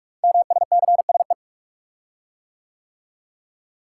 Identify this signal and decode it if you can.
Morse code, heard as a keyed tone, about 44 wpm, 700 Hz. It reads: MSCSE
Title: Code Groups - 5 Letter:Numbers - 45wpm